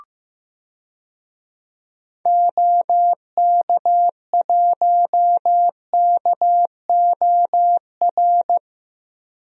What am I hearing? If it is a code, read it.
Morse code, heard as a keyed tone, about 15 wpm, 700 Hz. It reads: OK1KOR
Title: DXCC-100 Call Signs - 6 Characters - 15wpm